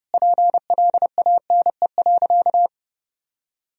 Morse code, heard as a keyed tone, about 29 wpm, 700 Hz. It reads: PLANE.
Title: Sentences from Top 1000 Words - Review Part 3 - 30wpm